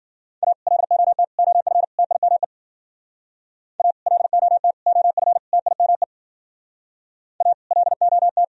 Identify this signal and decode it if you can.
Morse code, heard as a keyed tone, about 51 wpm, 700 Hz. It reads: A LOT OF TIME A LOT OF TIME A LOT